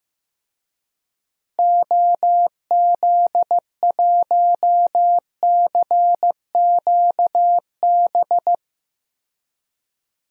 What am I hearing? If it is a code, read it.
Morse code, heard as a keyed tone, about 15 wpm, 700 Hz. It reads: OZ1CQB
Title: DXCC-100 Call Signs - 6 Characters - Repeated 3x - 15wpm